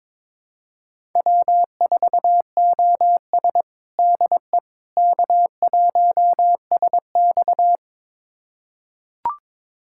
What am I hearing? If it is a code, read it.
Morse code, heard as a keyed tone, about 21 wpm, 700 Hz. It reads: W4OS DE K1SX
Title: USA Callsign DE Callsign - 4 Characters - 22wpm